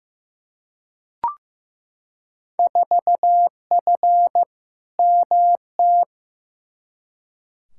Morse code, heard as a keyed tone, about 16 wpm, 700 Hz. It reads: 4F MT
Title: Field Day Exchanges Encore - 15wpm